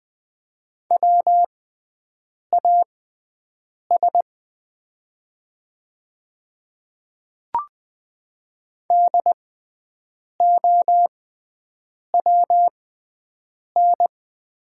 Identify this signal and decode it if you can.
Morse code, heard as a keyed tone, about 20 wpm, spaced as for 7 wpm, 690 Hz. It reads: WAS DOWN
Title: 6x Character Spacing - Top 100 Words - 20wpm